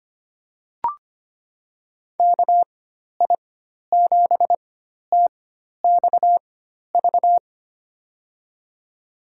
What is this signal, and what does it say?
Morse code, heard as a keyed tone, about 25 wpm, spaced as for 12 wpm, 700 Hz. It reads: KI7TXV
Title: USA Call Signs - 4x Character Spacing - 25wpm